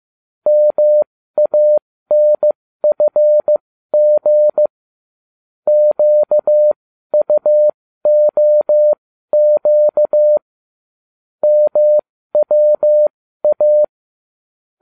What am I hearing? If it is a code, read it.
Morse code, heard as a keyed tone, about 15 wpm, spaced as for 12 wpm, 600 Hz. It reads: MANFG QUOQ MWA